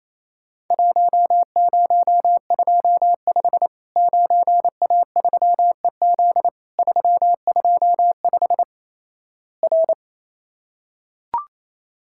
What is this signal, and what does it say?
Morse code, heard as a keyed tone, about 28 wpm, 700 Hz. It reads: 1025 9A3EZ 325 R
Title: Worked All Europe DX Contest - QTC Exchange - 28wpm